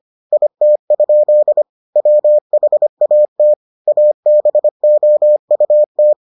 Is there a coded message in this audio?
Yes: Morse code from 0.3 to 6.1 s, heard as a keyed tone, about 25 wpm, 600 Hz. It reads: IT? WHAT ABOUT